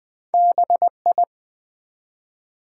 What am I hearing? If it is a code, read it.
Morse code, heard as a keyed tone, about 20 wpm, 700 Hz. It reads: BI
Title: Build A Word - 1 Letter At A Time - 4-Letter Words - 20wpm